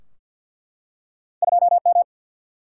Morse code, heard as a keyed tone, about 50 wpm, 700 Hz. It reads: 2 M
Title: Single-Letter Single-Number 50wpm